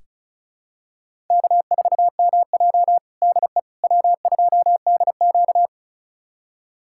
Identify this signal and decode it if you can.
Morse code, heard as a keyed tone, about 34 wpm, 700 Hz. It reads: K4MJ DE W2DQ
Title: USA Callsign DE Callsign - 4 Characters - 35wpm